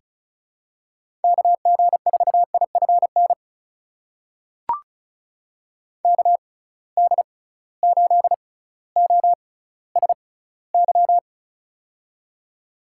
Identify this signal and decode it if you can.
Morse code, heard as a keyed tone, about 34 wpm, spaced as for 18 wpm, 700 Hz. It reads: KG4IFN KD8OSY
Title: USA Call Signs - 6 Characters - 6x Character Spacing - 35wpm